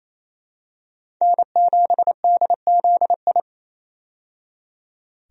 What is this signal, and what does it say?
Morse code, heard as a keyed tone, about 27 wpm, 700 Hz. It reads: N7DZI
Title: USA Call Signs - 5 Characters - 28wpm